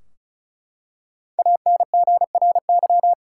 Morse code, heard as a keyed tone, about 36 wpm, 700 Hz. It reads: ANGRY
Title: Five Letter Words - 35wpm